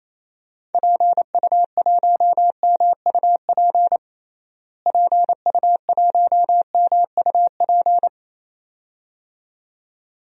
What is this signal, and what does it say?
Morse code, heard as a keyed tone, about 28 wpm, 700 Hz. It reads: PU1MUP PU1MUP
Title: DXCC-100 Call Signs - 6 Characters - Repeated 2x - 28wpm